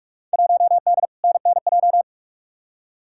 Morse code, heard as a keyed tone, about 43 wpm, 700 Hz. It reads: 1D NNJ
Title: Field Day Exchanges - 45wpm